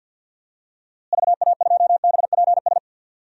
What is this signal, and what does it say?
Morse code, heard as a keyed tone, about 47 wpm, 700 Hz. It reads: VA2BPS
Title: DXCC-100 Call Signs - 6 Characters - Repeated 2x - 50wpm